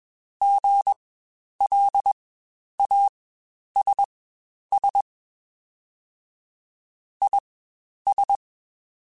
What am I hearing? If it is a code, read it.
Morse code, heard as a keyed tone, about 21 wpm, spaced as for 10 wpm, 800 Hz. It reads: GLASS IS
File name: Spaced / Chuckles Spaced-mp3s / HalfFull sp @20